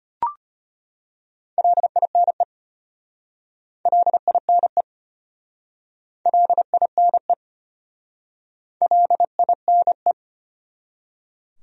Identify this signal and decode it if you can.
Morse code, heard as a keyed tone, about 33 wpm, 700 Hz. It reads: LINE LINE LINE LINE